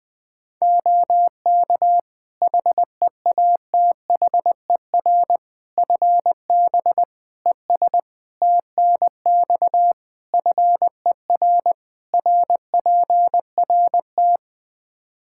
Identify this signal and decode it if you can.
Morse code, heard as a keyed tone, about 21 wpm, 700 Hz. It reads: OK HEATHER FB ES TNX FER RPRT
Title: K1USN SST QSO Practice 20wpm